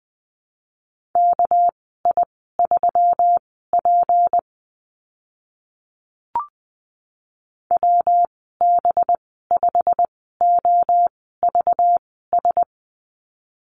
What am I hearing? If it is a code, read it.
Morse code, heard as a keyed tone, about 20 wpm, spaced as for 15 wpm, 700 Hz. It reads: KI3P WB5OVS
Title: 245 Lesson 39g - Letter z - Call Signs 20wpm